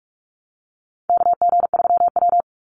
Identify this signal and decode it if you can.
Morse code, heard as a keyed tone, about 44 wpm, 700 Hz. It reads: KG3W
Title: NA SOTA Chaser Callsigns 45wpm